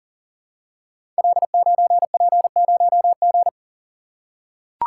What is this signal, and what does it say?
Morse code, heard as a keyed tone, about 38 wpm, 700 Hz. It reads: L9P0G
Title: Code Groups - 5 Letter:Numbers - 40wpm